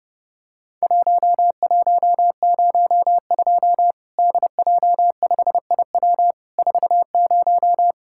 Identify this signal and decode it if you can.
Morse code, heard as a keyed tone, about 30 wpm, 700 Hz. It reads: 1102 DJ5IW 40
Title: Worked All Europe DX Contest - QTC Exchange - 30wpm